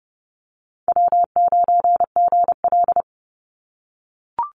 Morse code, heard as a keyed tone, about 30 wpm, 700 Hz. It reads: W9GL
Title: Call Signs - 30wpm